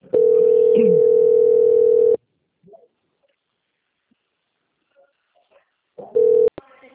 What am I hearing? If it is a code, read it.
Ringback tone (North America), 440 + 480 Hz, with speech.
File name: out- - - - - .85082